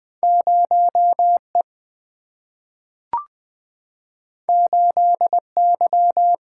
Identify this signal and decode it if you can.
Morse code, heard as a keyed tone, about 20 wpm, 700 Hz. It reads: 0 E 8 Y
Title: Single-Letter Single-Number 20wpm